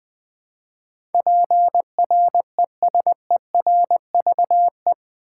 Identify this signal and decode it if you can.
Morse code, heard as a keyed tone, about 20 wpm, 700 Hz. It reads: PRESERVE
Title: Eight Letter Words - 20wpm